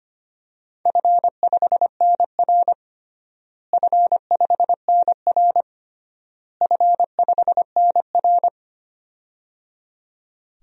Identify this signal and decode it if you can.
Morse code, heard as a keyed tone, about 25 wpm, 700 Hz. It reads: F5NR F5NR F5NR
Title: DXCC-100 Call Signs - 4 Characters - Repeated 3x - 25wpm